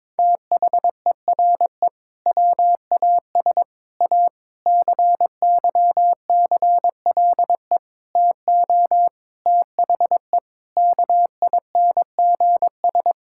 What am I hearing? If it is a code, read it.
Morse code, heard as a keyed tone, about 22 wpm, 700 Hz. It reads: THERE WAS A CYCLE TO THE KINGS
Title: Sentences from Top 1000 Words - Review Part 6 - 22wpm